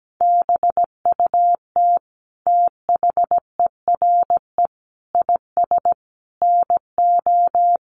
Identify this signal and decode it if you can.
Morse code, heard as a keyed tone, about 17 wpm, 700 Hz. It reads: BUT THERE IS NO
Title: Sentences from Top 1100 Words 17wpm